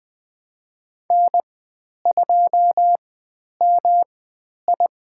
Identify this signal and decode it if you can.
Morse code, heard as a keyed tone, about 20 wpm, spaced as for 10 wpm, 700 Hz. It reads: N2MI
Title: 109 Lesson 18g - Letter m - Call Signs 20wpm